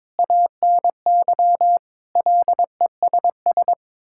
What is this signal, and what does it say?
Morse code, heard as a keyed tone, about 22 wpm, 700 Hz. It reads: ANY LESS
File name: practice / 22wpm / Sets of 7 Words - 22wpm